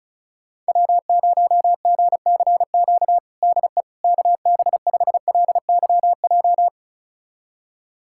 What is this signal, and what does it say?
Morse code, heard as a keyed tone, about 36 wpm, 700 Hz. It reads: W0GCQ DE KB5LYJ
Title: USA Callsign DE Callsign 35wpm